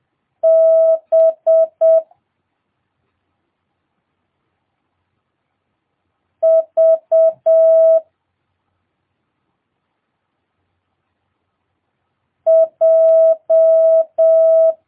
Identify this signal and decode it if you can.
Morse code, heard as a keyed tone, about 7 wpm, 650 Hz. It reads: B V J
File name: jh8cbh / 2022cw